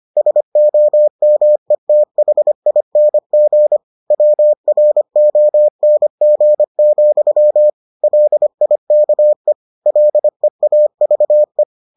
Morse code, heard as a keyed tone, about 25 wpm, 600 Hz. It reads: SOMETHING WRONG, LIKE LEAVE